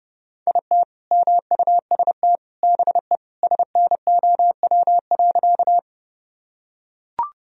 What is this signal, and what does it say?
Morse code, heard as a keyed tone, about 30 wpm, 700 Hz. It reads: IT MUST BE SNOW.
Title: Sentences from Top 1000 Words - Review Part 3 - 30wpm